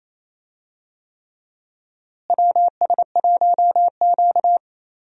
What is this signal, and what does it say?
Morse code, heard as a keyed tone, about 28 wpm, 700 Hz. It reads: WS1Q
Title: USA Call Signs - 4 Characters - Repeated 3x - 28wpm